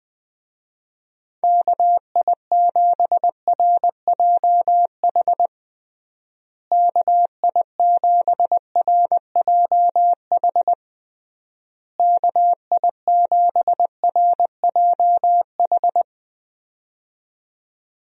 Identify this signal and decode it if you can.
Morse code, heard as a keyed tone, about 21 wpm, 700 Hz. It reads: KI7RJH KI7RJH KI7RJH
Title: USA Call Signs - 6 Characters - Repeated 3x - 20wpm